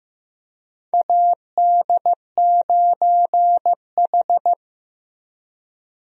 Morse code, heard as a keyed tone, about 15 wpm, 700 Hz. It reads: AD9H
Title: USA Call Signs - 4 Characters - Repeated 3x - 15wpm